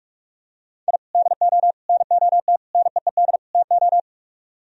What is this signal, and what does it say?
Morse code, heard as a keyed tone, about 45 wpm, 700 Hz. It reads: I DO NOT NEED TO